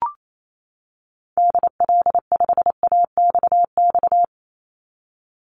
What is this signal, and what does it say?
Morse code, heard as a keyed tone, about 28 wpm, 700 Hz. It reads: DL5AXX
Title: Call Signs - 28wpm